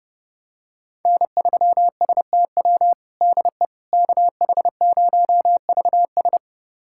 Morse code, heard as a keyed tone, about 30 wpm, 700 Hz. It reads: N3STW DE KH0VS
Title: USA Callsign DE Callsign - 5 Characters - 30wpm